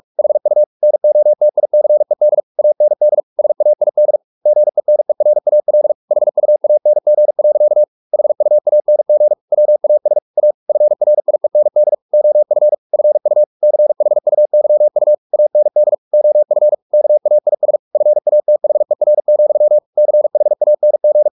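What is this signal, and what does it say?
Morse code, heard as a keyed tone, about 44 wpm, 600 Hz. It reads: HU NOTICED AND SAID GENERAL HUANG. HUANG WAS A FRIEND OF FU CHUYU AND OF KAIS FATHER, CHANG
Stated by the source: Morse practice at 45 wpm, the novel Never by Ken Follett in English, from CW med Gnister.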